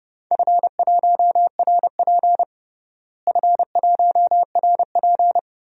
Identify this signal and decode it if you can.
Morse code, heard as a keyed tone, about 31 wpm, 700 Hz. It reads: F1RP F1RP
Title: DXCC-100 Call Signs - 4 Characters - Repeated 3x - 30wpm